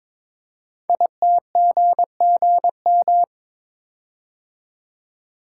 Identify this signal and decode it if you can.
Morse code, heard as a keyed tone, about 22 wpm, 700 Hz. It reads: ITGGM
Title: Code Groups - 5 Letters - 22wpm